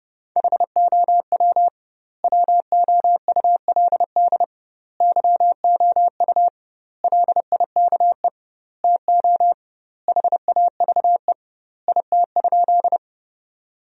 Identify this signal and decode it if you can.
Morse code, heard as a keyed tone, about 30 wpm, 700 Hz. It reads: HOW WOULD YOU LIKE TO HAVE IT?
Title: 2x Word-Spacing - Sentences from Top 100 Words - 30wpm